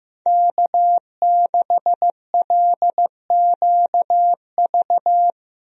Morse code, heard as a keyed tone, about 15 wpm, 700 Hz. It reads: K6LQV
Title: USA Call Signs - 5 Characters - Repeated 2x - 15wpm